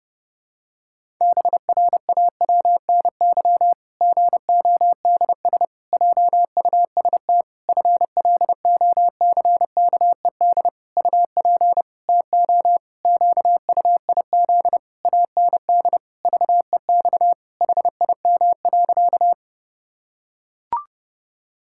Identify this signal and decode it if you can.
Morse code, heard as a keyed tone, about 30 wpm, 700 Hz. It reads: BRAWNY GODS JUST FLOCKED UP TO QUIZ AND VEX HIM.
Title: Pangram Challenge v2 - 30wpm